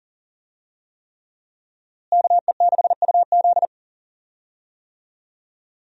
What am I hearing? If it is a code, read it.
Morse code, heard as a keyed tone, about 40 wpm, 700 Hz. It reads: KE6UZ
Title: USA Call Signs - 5 Characters - Repeated 3x - 40wpm